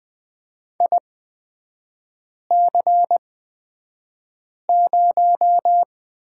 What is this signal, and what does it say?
Morse code, heard as a keyed tone, about 20 wpm, spaced as for 5 wpm, 700 Hz. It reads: IC0
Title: 224 Lesson 36e - BT - Three Characters 20wpm